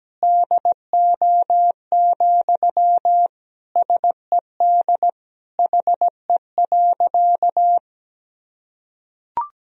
Morse code, heard as a keyed tone, about 17 wpm, 700 Hz. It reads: DO, SED HE.
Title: Sentences from Top 100 Words and Abbreviations 17wpm